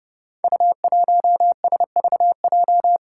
Morse code, heard as a keyed tone, about 30 wpm, 700 Hz. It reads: U1SVJ
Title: Five Challenging Letter-Number Combinations 30wpm